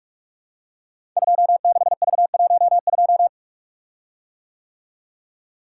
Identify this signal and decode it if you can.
Morse code, heard as a keyed tone, about 44 wpm, 700 Hz. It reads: 26V12
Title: Code Groups - 5 Letter:Numbers - 45wpm